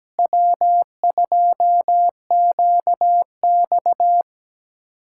Morse code, heard as a keyed tone, about 17 wpm, 700 Hz. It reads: W2QX
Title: USA Callsign DE Callsign - 4 Characters - 17wpm